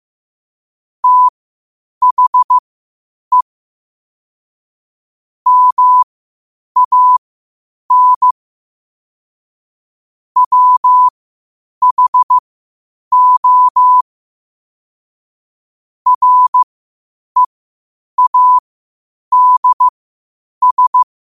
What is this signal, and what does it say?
Morse code, heard as a keyed tone, about 15 wpm, spaced as for 8 wpm, 1000 Hz. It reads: THE MAN WHO READS
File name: qotdfiles / 8wpm